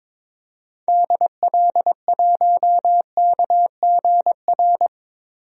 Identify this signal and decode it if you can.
Morse code, heard as a keyed tone, about 22 wpm, 700 Hz. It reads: DL1KGR
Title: DXCC-100 Call Signs - 6 Characters - Repeated 3x - 22wpm